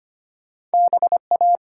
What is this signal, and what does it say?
Morse code, heard as a keyed tone, about 34 wpm, 700 Hz. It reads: BA